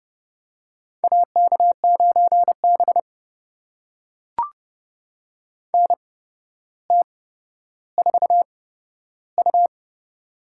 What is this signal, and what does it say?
Morse code, heard as a keyed tone, about 30 wpm, spaced as for 13 wpm, 700 Hz. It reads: AK9B NT4U
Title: USA Call Signs - 4 Characters - 8x Character Spacing - 30wpm